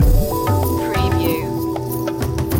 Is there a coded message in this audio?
No.